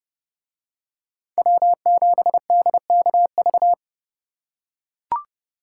Morse code, heard as a keyed tone, about 30 wpm, 700 Hz. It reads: W7DKV